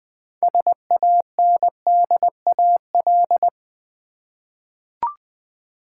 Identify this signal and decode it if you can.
Morse code, heard as a keyed tone, about 20 wpm, 700 Hz. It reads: SANDAL
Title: Easy Words with Hints - 20wpm